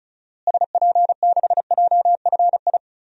Morse code, heard as a keyed tone, about 34 wpm, 700 Hz. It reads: SP6JFI
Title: DXCC-100 Call Signs - 6 Characters - Repeated 2x - 35wpm